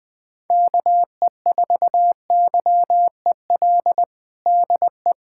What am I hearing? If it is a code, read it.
Morse code, heard as a keyed tone, about 20 wpm, 700 Hz. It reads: KE4YEL DE
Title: USA Callsign DE Callsign 20wpm